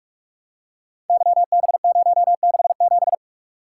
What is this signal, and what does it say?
Morse code, heard as a keyed tone, about 44 wpm, 700 Hz. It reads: YB067
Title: Code Groups - 5 Letter:Numbers - 45wpm